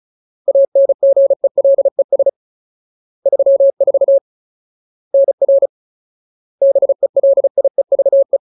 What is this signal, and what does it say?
Morse code, heard as a keyed tone, about 34 wpm, 550 Hz. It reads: ANGELES 34 NR BELIEVE
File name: Ultimate Word List / 35WPM Ultimate